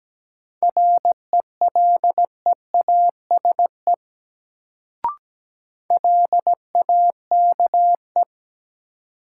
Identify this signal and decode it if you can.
Morse code, heard as a keyed tone, about 17 wpm, 700 Hz. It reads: RELEASE LAKE
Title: Sentences from Top 1100 Words - New Word Drill Alt - 17wpm